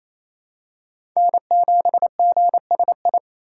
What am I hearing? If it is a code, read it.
Morse code, heard as a keyed tone, about 28 wpm, 700 Hz. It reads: N7GSI
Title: USA Call Signs - 5 Characters - 4x Character Spacing - 28wpm